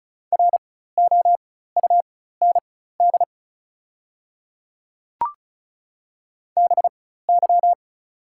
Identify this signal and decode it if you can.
Morse code, heard as a keyed tone, about 34 wpm, spaced as for 16 wpm, 700 Hz. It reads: ROUND BY
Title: 4x Character Spacing - Top 300 Words - 35wpm